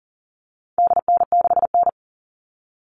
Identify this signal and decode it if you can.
Morse code, heard as a keyed tone, about 40 wpm, 700 Hz. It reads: DN6N
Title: All Cut Numbers - 4-Digit 40wpm